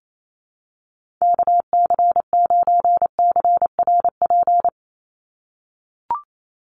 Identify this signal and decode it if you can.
Morse code, heard as a keyed tone, about 28 wpm, 700 Hz. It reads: KC9CRP
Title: USA Call Signs 28wpm